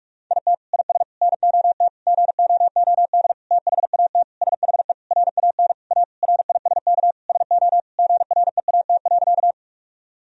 Morse code, heard as a keyed tone, about 46 wpm, 700 Hz. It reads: IT IS NOT GOOD THAT SHE RAN A RISK SO GREAT.
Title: Sentences from Top 1000 Words - Review Part 4 - 45wpm